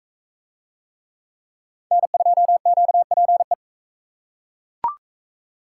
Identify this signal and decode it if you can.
Morse code, heard as a keyed tone, about 41 wpm, 700 Hz. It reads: N2QPE